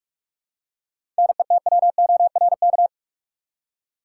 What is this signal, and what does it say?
Morse code, heard as a keyed tone, about 45 wpm, 700 Hz. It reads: NETWORK
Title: Seven Letter Words - 45wpm